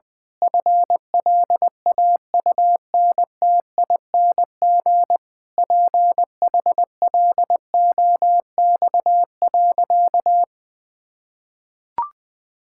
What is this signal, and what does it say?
Morse code, heard as a keyed tone, about 21 wpm, 700 Hz. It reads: FLAUNTING PHLOX.
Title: Pangram Challenge v2 - 20wpm